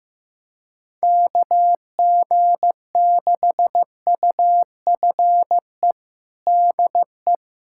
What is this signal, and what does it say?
Morse code, heard as a keyed tone, about 15 wpm, 700 Hz. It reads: KG6UFE DE
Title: USA Callsign DE Callsign 15wpm